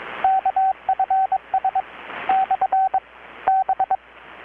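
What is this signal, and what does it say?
Morse code, heard as a keyed tone, about 21 wpm, 760 Hz, with radio noise. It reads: KFS / B
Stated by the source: Morse from KPH's Night of Nights XX maritime radiotelegraphy event, received in Chiapas, Mexico about 3,800 km away on a Kenwood R-600 with a grounded 10 m long-wire antenna.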